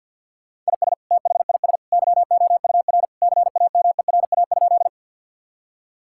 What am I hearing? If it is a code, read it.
Morse code, heard as a keyed tone, about 51 wpm, 700 Hz. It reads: IS THIS YOUR CAMERA?
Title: Sentences from Top 1000 Words - Review Part 5 - 50wpm